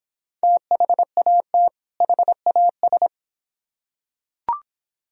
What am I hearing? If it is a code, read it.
Morse code, heard as a keyed tone, about 26 wpm, 700 Hz. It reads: THAT HAS